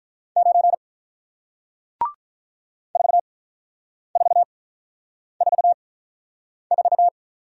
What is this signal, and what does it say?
Morse code, heard as a keyed tone, about 48 wpm, spaced as for 9 wpm, 700 Hz. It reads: 9 4444